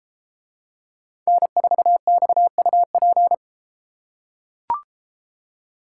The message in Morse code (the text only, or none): N4XUP